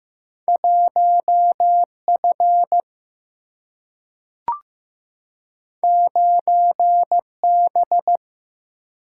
Morse code, heard as a keyed tone, about 15 wpm, 700 Hz. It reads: 1F 9B
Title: Single-Letter Single-Number 15wpm